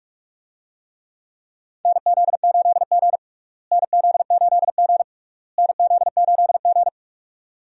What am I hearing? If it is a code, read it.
Morse code, heard as a keyed tone, about 46 wpm, 700 Hz. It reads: NZ8G NZ8G NZ8G
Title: USA Call Signs - 4 Characters - Repeated 3x - 45wpm